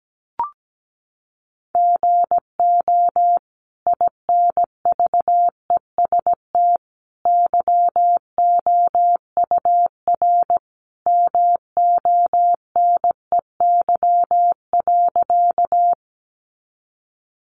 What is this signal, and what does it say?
Morse code, heard as a keyed tone, about 17 wpm, 700 Hz. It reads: GO INVEST YOUR MONEY.
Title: Sentences from Top 1100 Words 17wpm